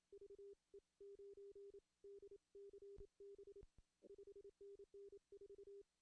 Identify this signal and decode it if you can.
Morse code, heard as a keyed tone, about 27 wpm, spaced as for 24 wpm, 400 Hz. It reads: VE9DCB 5NN 4
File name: CW / cqww-cw-k5tr